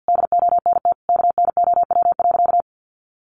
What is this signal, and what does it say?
Morse code, heard as a keyed tone, about 50 wpm, 700 Hz. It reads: DONT KNOW.
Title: Sentences from Top 600 Words 50wpm